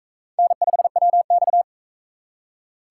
Morse code, heard as a keyed tone, about 41 wpm, 700 Hz. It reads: N5WX